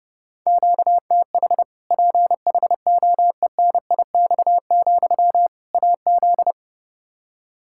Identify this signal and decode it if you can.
Morse code, heard as a keyed tone, about 30 wpm, 700 Hz. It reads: QTH PHOENIX, AZ
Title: Practice QSO - Exchange 1 - Beginning - 30wpm